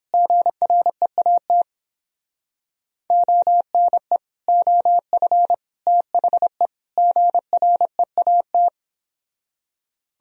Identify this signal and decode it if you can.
Morse code, heard as a keyed tone, about 29 wpm, 700 Hz. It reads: GREAT ONE OF THE GREAT